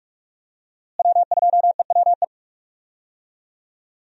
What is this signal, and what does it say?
Morse code, heard as a keyed tone, about 46 wpm, 700 Hz. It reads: W2EWE
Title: USA Call Signs - 5 Characters - Repeated 3x - 45wpm